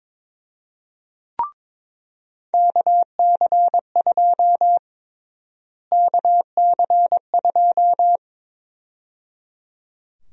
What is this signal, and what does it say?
Morse code, heard as a keyed tone, about 22 wpm, 700 Hz. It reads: KC2 KC2
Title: USA Call Sign Prefixes - Repeated 2x - 22wpm